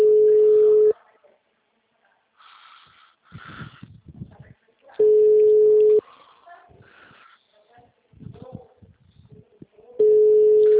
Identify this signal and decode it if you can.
Ringback tone (Europe, 425 Hz), with speech.